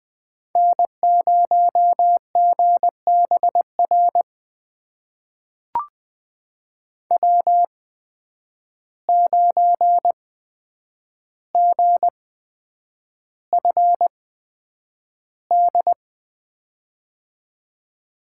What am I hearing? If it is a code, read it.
Morse code, heard as a keyed tone, about 20 wpm, spaced as for 9 wpm, 700 Hz. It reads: N0GBR W9GFD
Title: USA Call Signs - 8x Character Spacing - 20wpm